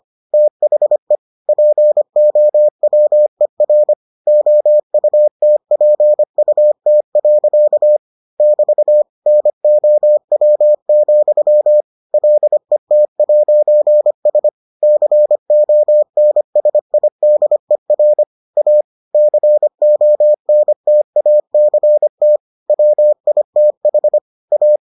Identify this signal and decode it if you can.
Morse code, heard as a keyed tone, about 25 wpm, 600 Hz. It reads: THE POWER OUTPUT. <BT> NOW, LET'S CONSIDER A CONTACT WITH A